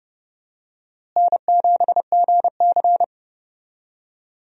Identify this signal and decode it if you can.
Morse code, heard as a keyed tone, about 30 wpm, 700 Hz. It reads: N7GC
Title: USA Call Signs - 4 Characters - Repeated 2x - 30wpm